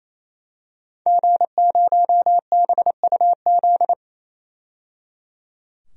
Morse code, heard as a keyed tone, about 27 wpm, 700 Hz. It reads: G0BUZ
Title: DXCC-100 Call Signs - 5 Characters - 28wpm